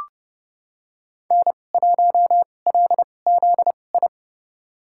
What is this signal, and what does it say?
Morse code, heard as a keyed tone, about 31 wpm, spaced as for 22 wpm, 700 Hz. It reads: N1LZI
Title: USA Call Signs - 5 Characters - 2x Character Spacing - 30wpm